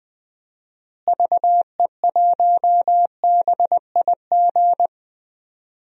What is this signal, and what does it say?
Morse code, heard as a keyed tone, about 20 wpm, 700 Hz. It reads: VE1BIG
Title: DXCC-100 Call Signs - 6 Characters - Repeated 3x - 20wpm